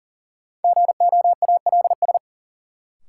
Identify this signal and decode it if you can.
Morse code, heard as a keyed tone, about 40 wpm, 700 Hz. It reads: GOALS
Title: Sentences from Top 1000 Words - Review Part 6 - New Word Drill - 40wpm